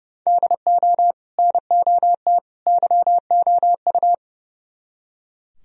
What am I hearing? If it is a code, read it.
Morse code, heard as a keyed tone, about 36 wpm, 700 Hz. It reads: DO NOT YOU
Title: Speed-Racing - Sets of 3 Words 30wpm